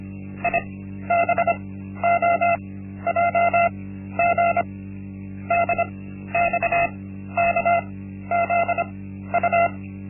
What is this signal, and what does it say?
Morse code, heard as a keyed tone, about 25 wpm, spaced as for 13 wpm, 690 Hz. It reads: IBOJG DXKZU
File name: Linksmakalnis-1